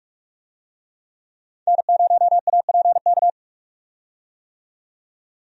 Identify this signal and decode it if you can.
Morse code, heard as a keyed tone, about 46 wpm, 700 Hz. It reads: N0APK
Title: USA Call Signs - 5 Characters - 45wpm